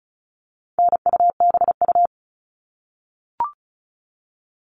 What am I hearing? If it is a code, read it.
Morse code, heard as a keyed tone, about 34 wpm, 700 Hz. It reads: NUBU
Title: All Cut Numbers - 4-Digit 35wpm